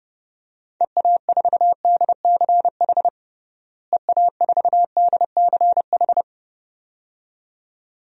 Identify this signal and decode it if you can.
Morse code, heard as a keyed tone, about 30 wpm, 700 Hz. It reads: EA4DCH EA4DCH
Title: DXCC-100 Call Signs - 6 Characters - Repeated 2x - 30wpm